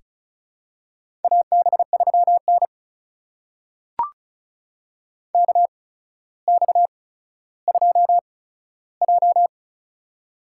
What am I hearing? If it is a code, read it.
Morse code, heard as a keyed tone, about 36 wpm, spaced as for 15 wpm, 700 Hz. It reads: AB3N KX2J
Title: USA Call Signs - 4 Characters - 8x Character Spacing - 35wpm